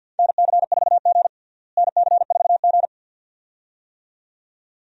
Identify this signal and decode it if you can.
Morse code, heard as a keyed tone, about 52 wpm, 700 Hz. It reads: NC4G NC4G
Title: USA Call Signs - 4 Characters - Repeated 2x - 50wpm